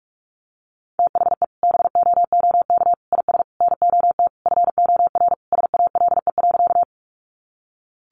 Morse code, heard as a keyed tone, about 44 wpm, 700 Hz. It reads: THE BOOK IS NOT FOR SALE.
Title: Sentences from Top 1400 Words 45wpm